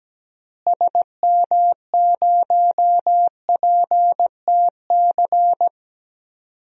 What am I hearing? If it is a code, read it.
Morse code, heard as a keyed tone, about 17 wpm, 700 Hz. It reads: SM0PTC
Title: DXCC-100 Call Signs - 6 Characters - Repeated 3x - 17wpm